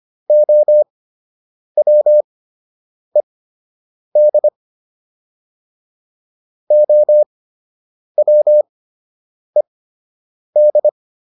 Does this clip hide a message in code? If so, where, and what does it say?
Yes: Morse code from 0.3 to 10.9 s, heard as a keyed tone, about 25 wpm, spaced as for 8 wpm, 600 Hz. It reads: OWED OWED